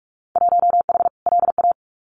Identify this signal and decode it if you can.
Morse code, heard as a keyed tone, about 46 wpm, 700 Hz. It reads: 1H LA
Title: Winter Field Day Exchanges 45wpm